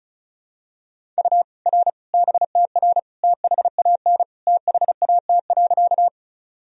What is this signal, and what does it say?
Morse code, heard as a keyed tone, about 36 wpm, 700 Hz. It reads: U R BTR THAN THAT.
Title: Sentences from Top 400 Words and Abbreviations 35wpm